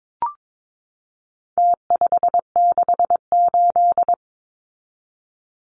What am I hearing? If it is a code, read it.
Morse code, heard as a keyed tone, about 22 wpm, 700 Hz. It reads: T568
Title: Numbers - Contest Style - 4-Digit 22wpm